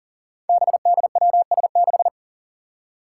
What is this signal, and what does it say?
Morse code, heard as a keyed tone, about 40 wpm, 700 Hz. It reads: BDWS6
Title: Five Challenging Letter-Number Combinations 40wpm